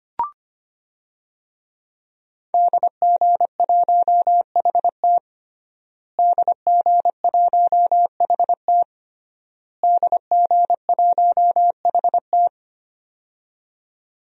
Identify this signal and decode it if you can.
Morse code, heard as a keyed tone, about 25 wpm, 700 Hz. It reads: DG1HT DG1HT DG1HT
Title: DXCC-100 Call Signs - 5 Characters - Repeated 3x - 25wpm